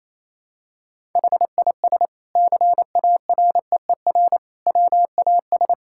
Morse code, heard as a keyed tone, about 28 wpm, 700 Hz. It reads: HIS CAREER WAS
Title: Sentences from Top 1000 Words - Review Part 6 - 28wpm